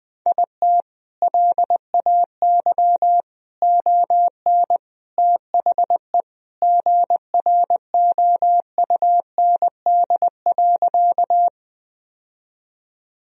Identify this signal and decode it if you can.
Morse code, heard as a keyed tone, about 20 wpm, 700 Hz. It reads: IT LAY ON THE GROUND.
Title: Sentences from Top 1000 Words - Review Part 2 - 20wpm